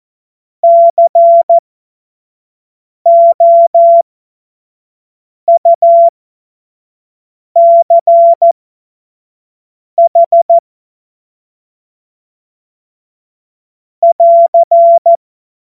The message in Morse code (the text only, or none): COUCH <AR>